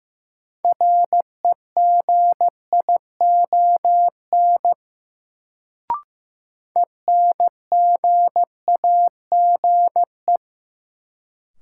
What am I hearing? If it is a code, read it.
Morse code, heard as a keyed tone, about 15 wpm, 700 Hz. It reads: REGION ENGAGE
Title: Sentences from Top 1500 Words - New Word Drill - 15wpm